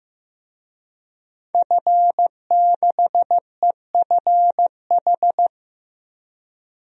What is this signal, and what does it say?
Morse code, heard as a keyed tone, about 15 wpm, 700 Hz. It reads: F6EFH
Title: DXCC-100 Call Signs - 5 Characters - 15wpm